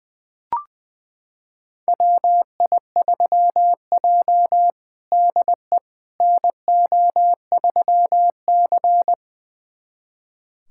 Morse code, heard as a keyed tone, about 20 wpm, 700 Hz. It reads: WI3J DE NO3C
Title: USA Callsign DE Callsign - 4 Characters - 20wpm